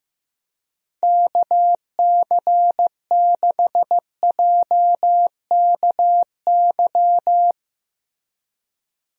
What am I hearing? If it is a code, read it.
Morse code, heard as a keyed tone, about 15 wpm, 700 Hz. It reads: KC6JKY
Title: USA Call Signs - 6 Characters - Repeated 2x - 15wpm